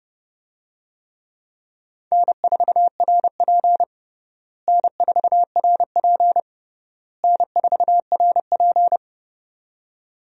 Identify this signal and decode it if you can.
Morse code, heard as a keyed tone, about 30 wpm, 700 Hz. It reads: N4RP N4RP N4RP
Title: USA Call Signs - 4 Characters - Repeated 3x - 30wpm